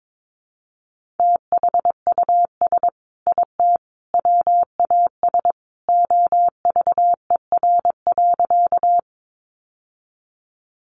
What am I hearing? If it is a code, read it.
Morse code, heard as a keyed tone, about 22 wpm, 700 Hz. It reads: THUS IT WAS OVER.
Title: Sentences from Top 800 Words 22wpm